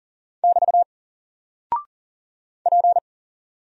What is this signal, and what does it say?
Morse code, heard as a keyed tone, about 38 wpm, 700 Hz. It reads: <BT> P
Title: Single Character (Letter:Number:Punctuation:Prosign) Alt 40wpm